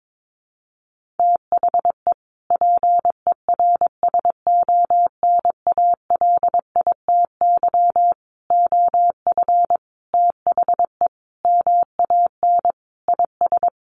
Morse code, heard as a keyed tone, about 22 wpm, 700 Hz. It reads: THE PERSONALITY OF THE MAN IS
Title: Sentences from Top 1400 Words 22wpm